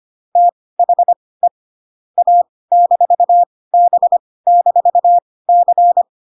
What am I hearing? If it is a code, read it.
Morse code, heard as a keyed tone, about 25 wpm, spaced as for 18 wpm, 700 Hz. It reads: THE A-B-C